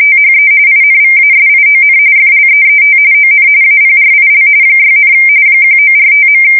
RTTY, about 45.45 baud. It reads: THE INVASION FLEET ON APRIL 6, 1945.
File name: fsk170 / USS-New-Jersey